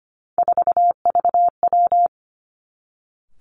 Morse code, heard as a keyed tone, about 25 wpm, 700 Hz. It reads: 4VW
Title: Three Challenging Letter-Number Combinations 25wpm